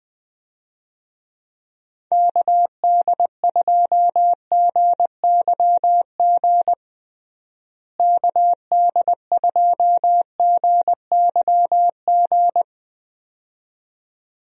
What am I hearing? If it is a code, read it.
Morse code, heard as a keyed tone, about 21 wpm, 700 Hz. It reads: KD2GYG KD2GYG
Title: USA Call Signs - 6 Characters - Repeated 2x - 20wpm